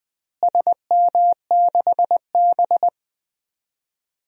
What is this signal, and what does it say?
Morse code, heard as a keyed tone, about 21 wpm, 700 Hz. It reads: SM6B
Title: DXCC-100 Call Signs - 4 Characters - Repeated 3x - 20wpm